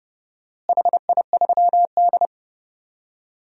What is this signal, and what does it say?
Morse code, heard as a keyed tone, about 31 wpm, 700 Hz. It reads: HI3D
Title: DXCC-100 Call Signs - 4 Characters - Repeated 3x - 30wpm